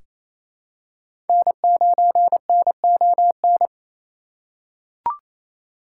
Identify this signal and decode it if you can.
Morse code, heard as a keyed tone, about 28 wpm, 700 Hz. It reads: N9NON
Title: USA Call Signs - 3x Character Spacing - 28wpm